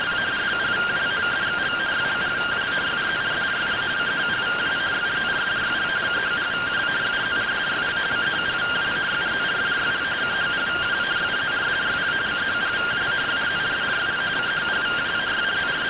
Two-tone teleprinter signal, about 170 Hz shift.
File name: findings / 100 surview 0.gen-37 rtty rx